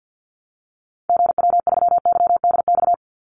Morse code, heard as a keyed tone, about 50 wpm, 700 Hz. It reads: GW3YDX
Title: Call Signs - 50wpm